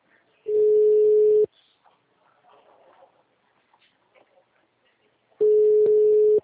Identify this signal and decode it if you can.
Ringback tone (Europe, 425 Hz).